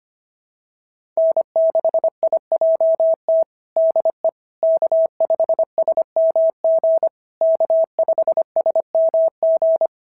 Morse code, heard as a keyed tone, about 24 wpm, 650 Hz. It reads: N6IJT DE K5SMG K5SMG
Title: SOTA QSO Practice 25wpm